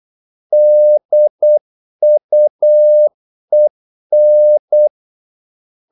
Morse code, heard as a keyed tone, about 8 wpm, 600 Hz. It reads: DUEN
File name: Korte ord DK 8wpm